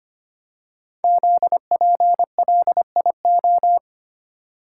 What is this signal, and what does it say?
Morse code, heard as a keyed tone, about 25 wpm, 700 Hz. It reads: ZPLIO
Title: Code Groups - 5 Letters - 25wpm